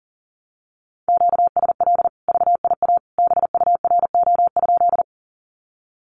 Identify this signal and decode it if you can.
Morse code, heard as a keyed tone, about 37 wpm, 700 Hz. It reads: QSL VIA BURO?
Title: Common QSO Phrases 40wpm